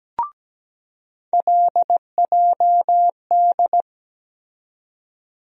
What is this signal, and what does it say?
Morse code, heard as a keyed tone, about 17 wpm, 700 Hz. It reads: LJD
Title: USA Call Sign Postfixes - 3 Letters 17wpm